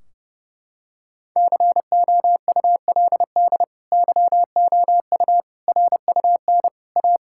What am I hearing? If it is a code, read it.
Morse code, heard as a keyed tone, about 30 wpm, 700 Hz. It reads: COULD YOU RUN A
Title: Sentences from Top 1600 Words 30wpm